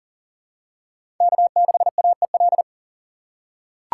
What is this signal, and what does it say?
Morse code, heard as a keyed tone, about 40 wpm, 700 Hz. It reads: K6AEL
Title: USA Call Signs - 5 Characters - 4x Character Spacing - 40wpm